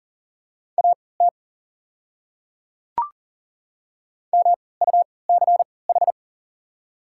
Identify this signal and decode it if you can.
Morse code, heard as a keyed tone, about 37 wpm, spaced as for 22 wpm, 690 Hz. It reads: AT MUCH
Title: 3x Character Spacing - Top 200 Words - 40wpm